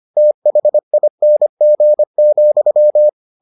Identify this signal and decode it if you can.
Morse code, heard as a keyed tone, about 25 wpm, 600 Hz. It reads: THING,